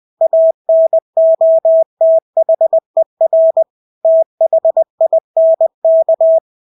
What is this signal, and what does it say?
Morse code, heard as a keyed tone, about 20 wpm, 650 Hz. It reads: ANOTHER THINK
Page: Engelske ord 20 wpm